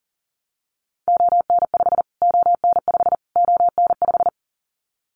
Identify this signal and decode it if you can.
Morse code, heard as a keyed tone, about 40 wpm, 700 Hz. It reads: ON5 ON5 ON5
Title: Callsign Prefixes for Top 25 DXCC - Repeated 3x - 40wpm